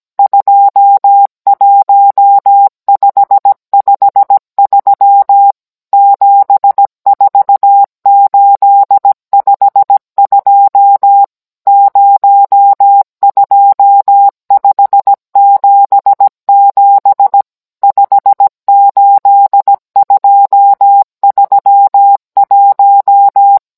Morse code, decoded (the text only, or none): 21553 74852 02577 58231